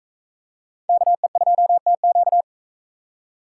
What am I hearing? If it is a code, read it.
Morse code, heard as a keyed tone, about 40 wpm, 700 Hz. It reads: KE2TQ